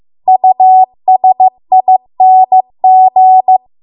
Morse code, heard as a keyed tone, about 15 wpm, 750 Hz. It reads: USING